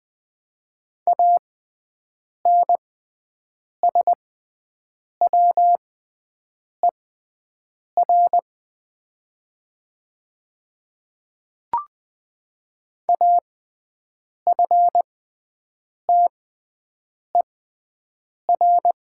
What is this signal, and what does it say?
Morse code, heard as a keyed tone, about 20 wpm, spaced as for 7 wpm, 700 Hz. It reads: ANSWER AFTER
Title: 6x Character Spacing - Top 500 Words - 20wpm